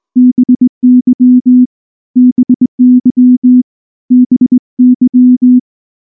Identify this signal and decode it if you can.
Morse code, heard as a keyed tone, about 18 wpm, 250 Hz. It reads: BY BY BY
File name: morse / morse-training-20WPM-1